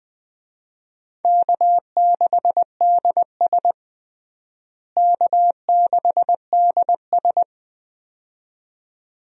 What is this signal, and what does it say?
Morse code, heard as a keyed tone, about 21 wpm, 700 Hz. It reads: K6DS K6DS
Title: USA Call Signs - 4 Characters - Repeated 2x - 20wpm